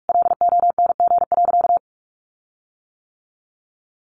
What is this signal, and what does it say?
Morse code, heard as a keyed tone, about 44 wpm, 700 Hz. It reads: LONG.
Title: Sentences from Top 1300 Words 45wpm